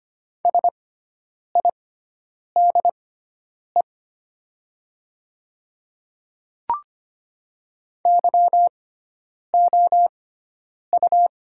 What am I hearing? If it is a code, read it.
Morse code, heard as a keyed tone, about 25 wpm, spaced as for 9 wpm, 700 Hz. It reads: SIDE YOU
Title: 6x Character Spacing - Top 200 Words - 25wpm